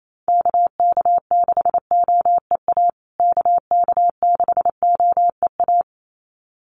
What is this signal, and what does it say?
Morse code, heard as a keyed tone, about 28 wpm, 700 Hz. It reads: KK6OEA KK6OEA
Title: USA Call Signs Repeated 2x 28wpm